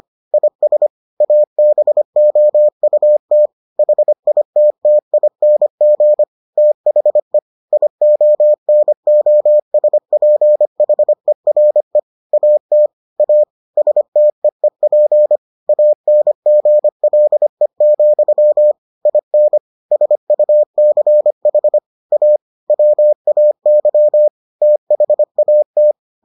Morse code, heard as a keyed tone, about 25 wpm, 600 Hz. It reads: IS ABOUT HITTING THE IONOSPHERE AT A STEEP ANGLE, IN SUCH A WAY THAT